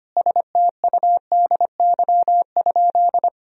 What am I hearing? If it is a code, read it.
Morse code, heard as a keyed tone, about 25 wpm, 700 Hz. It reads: STUDY?
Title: Sentences from Top 1000 Words - Review Part 1 - 25wpm